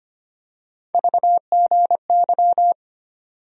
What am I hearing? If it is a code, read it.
Morse code, heard as a keyed tone, about 25 wpm, 700 Hz. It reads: VGY
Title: USA Call Sign Postfixes - 3 Letters 25wpm